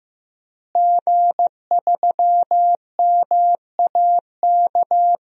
Morse code, heard as a keyed tone, about 15 wpm, 700 Hz. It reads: G3MAK
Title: DXCC-100 Call Signs - 5 Characters - Repeated 2x - 15wpm